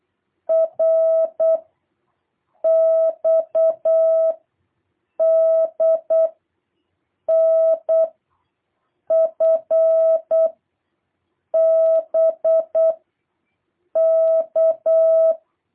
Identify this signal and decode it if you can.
Morse code, heard as a keyed tone, about 8 wpm, 660 Hz. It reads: R X D N F B K
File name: jh8cbh / 2023cw